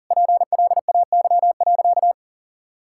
Morse code, heard as a keyed tone, about 40 wpm, 700 Hz. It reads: PLAY.
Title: Sentences from Top 1000 Words - Review Part 6 - 40wpm